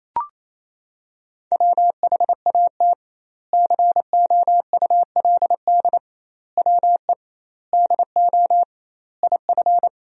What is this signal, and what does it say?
Morse code, heard as a keyed tone, about 28 wpm, 700 Hz. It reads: WHAT COULD WE DO IF
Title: 2x Word-Spacing - Sentences from Top 400 Words - 28wpm